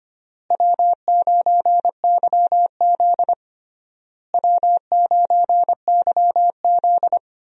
Morse code, heard as a keyed tone, about 25 wpm, 700 Hz. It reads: W9YZ W9YZ
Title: USA Call Signs - 4 Characters - Repeated 3x - 25wpm